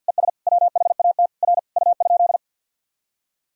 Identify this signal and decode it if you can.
Morse code, heard as a keyed tone, about 53 wpm, 700 Hz. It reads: ES WHAT R U?
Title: Sentences from Top 100 Words and Abbreviations 50wpm